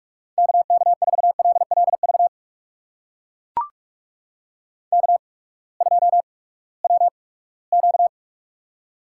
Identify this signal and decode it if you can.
Morse code, heard as a keyed tone, about 44 wpm, spaced as for 22 wpm, 700 Hz. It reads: KK4LLV K2WQ
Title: USA Call Signs - 8x Character Spacing - 45wpm